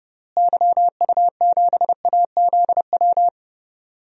Morse code, heard as a keyed tone, about 31 wpm, 700 Hz. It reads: YU7AZW
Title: DXCC-100 Call Signs - 6 Characters - Repeated 3x - 30wpm